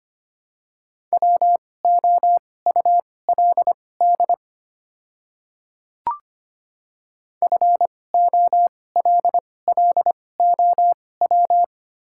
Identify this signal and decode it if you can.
Morse code, heard as a keyed tone, about 25 wpm, spaced as for 18 wpm, 690 Hz. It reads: WOULD FOLLOW
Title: 2x Character Spacing - Top 200 Words - 25wpm